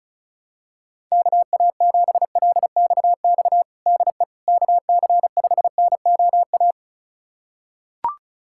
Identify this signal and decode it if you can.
Morse code, heard as a keyed tone, about 36 wpm, 700 Hz. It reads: KA7LXX DE KC5NOA
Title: USA Callsign DE Callsign 35wpm